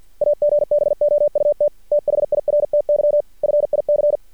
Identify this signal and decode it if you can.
Morse code, heard as a keyed tone, about 50 wpm, 590 Hz. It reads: AGBOUT THIRTY FIC
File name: archived / CRSnet